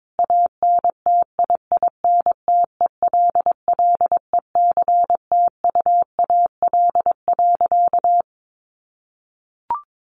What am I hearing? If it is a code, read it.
Morse code, heard as a keyed tone, about 22 wpm, 700 Hz. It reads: ANTIINTELLECTUAL.
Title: A Sound Of Thunder 22wpm